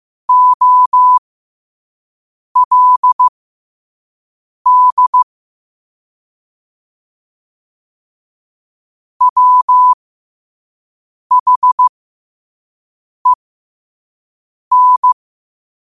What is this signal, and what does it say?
Morse code, heard as a keyed tone, about 15 wpm, spaced as for 5 wpm, 1000 Hz. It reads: OLD WHEN
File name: qotdfiles / 5wpm